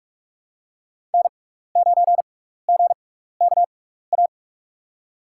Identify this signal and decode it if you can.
Morse code, heard as a keyed tone, about 44 wpm, spaced as for 15 wpm, 690 Hz. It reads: N9GKA
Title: USA Call Signs - 5 Characters - 6x Character Spacing - 45wpm